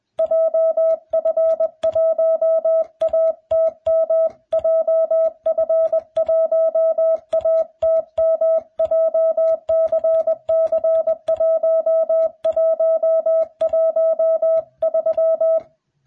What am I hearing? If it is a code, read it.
Morse code, heard as a keyed tone, about 21 wpm, 660 Hz. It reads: JF1ATM JF1ATM JCC1113